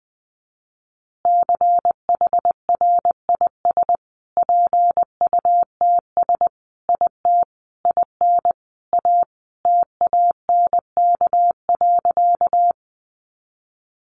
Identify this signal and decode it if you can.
Morse code, heard as a keyed tone, about 20 wpm, 700 Hz. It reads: CHRIS PUTS IT IN A TANK.
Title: Easy Phrases 20wpm